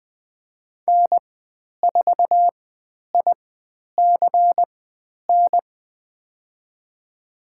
Morse code, heard as a keyed tone, about 20 wpm, spaced as for 10 wpm, 700 Hz. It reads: N4ICN
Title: 095 Lesson 16g - Letter c - Call Signs 20wpm